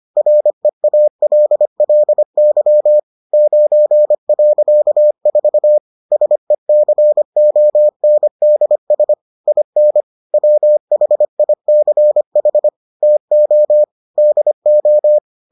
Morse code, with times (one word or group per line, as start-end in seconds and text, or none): 0.2-3.0 REALLY
3.3-5.8 9.4
6.1-9.1 SECONDS
9.5-10.0 IN
10.3-12.7 WHICH
13.0-13.8 TO
14.2-15.2 DO